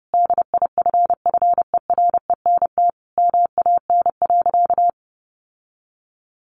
Morse code, heard as a keyed tone, about 30 wpm, 700 Hz. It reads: DIFFERENT MAN.
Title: Sentences from Top 600 Words 30wpm